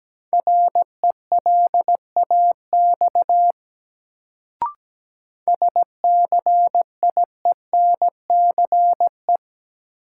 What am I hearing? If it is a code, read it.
Morse code, heard as a keyed tone, about 17 wpm, 700 Hz. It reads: RELAX SCIENCE
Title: Sentences from Top 1400 Words - New Word Drill Alt - 17wpm